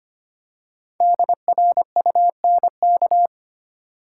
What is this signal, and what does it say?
Morse code, heard as a keyed tone, about 25 wpm, 700 Hz. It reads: DRUNK
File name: 5 Letter Words - First 3 Letters Given - 25wpm